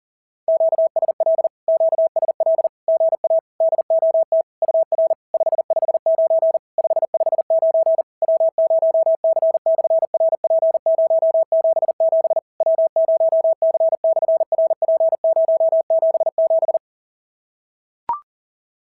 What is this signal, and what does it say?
Morse code, heard as a keyed tone, about 40 wpm, 690 Hz. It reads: QSL QSL GA DOT UR 559 559 W0C/RP077 W0C/RP077
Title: SOTA QSO Practice 40wpm